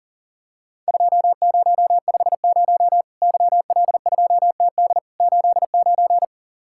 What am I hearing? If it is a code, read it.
Morse code, heard as a keyed tone, about 40 wpm, 700 Hz. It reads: 2050 YL2TD 89
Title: Worked All Europe DX Contest - QTC Exchange - 40wpm